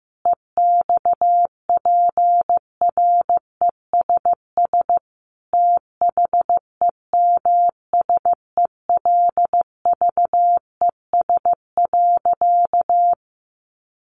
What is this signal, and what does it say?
Morse code, heard as a keyed tone, about 15 wpm, 700 Hz. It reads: EXPRESS THEMSELVES.
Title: Sentences from Top 1100 Words 15wpm